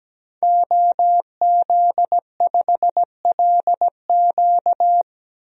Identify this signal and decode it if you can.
Morse code, heard as a keyed tone, about 17 wpm, 700 Hz. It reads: OZ5LQ
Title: DXCC-100 Call Signs - 5 Characters - 17wpm